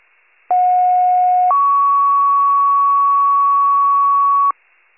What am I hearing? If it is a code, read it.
Two-tone page, heard as a one-second tone, then a three-second tone, about 730 then 1120 Hz. It is heard from 0.5 to 4.5 s.
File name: pager / 83 RESCUE OFC / wcalert